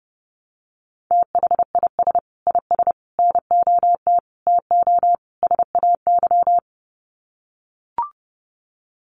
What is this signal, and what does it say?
Morse code, heard as a keyed tone, about 30 wpm, 700 Hz. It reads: THIS IS NOT TO SAY
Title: Sets of 5 Words Encore 30wpm